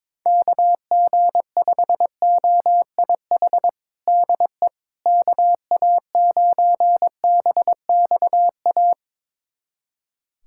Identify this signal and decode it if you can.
Morse code, heard as a keyed tone, about 22 wpm, 700 Hz. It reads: KG5OIH DE KA9BXA
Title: USA Callsign DE Callsign - 6 Characters - 22wpm